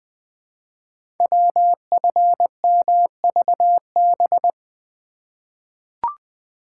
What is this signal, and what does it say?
Morse code, heard as a keyed tone, about 20 wpm, 700 Hz. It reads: WFMVB
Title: Code Groups - 5 Letters - 20wpm